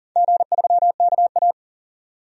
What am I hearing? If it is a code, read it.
Morse code, heard as a keyed tone, about 40 wpm, 690 Hz. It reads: G3KA
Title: DXCC-100 Call Signs - 4 Characters - 40wpm